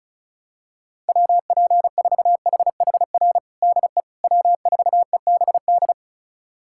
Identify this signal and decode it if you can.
Morse code, heard as a keyed tone, about 34 wpm, 700 Hz. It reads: WP4HHR DE W4EBD
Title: USA Callsign DE Callsign 35wpm